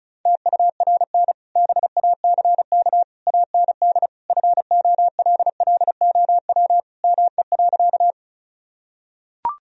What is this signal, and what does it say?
Morse code, heard as a keyed tone, about 36 wpm, 700 Hz. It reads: TURN BACK AND FOLLOW ME.
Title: Sentences from Top 1000 Words - Review Part 1 - 35wpm